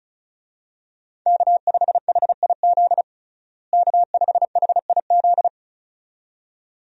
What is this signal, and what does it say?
Morse code, heard as a keyed tone, about 36 wpm, 700 Hz. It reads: K5HIZ K5HIZ
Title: USA Call Signs - 5 Characters - Repeated 2x - 35wpm